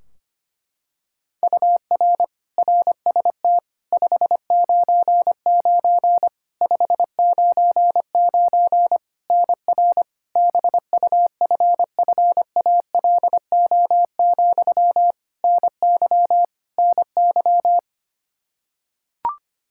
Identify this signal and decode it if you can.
Morse code, heard as a keyed tone, about 25 wpm, 700 Hz. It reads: UR RST 599 599 NR BUFFALO, NY NY
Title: K1USN SST QSO Practice 25wpm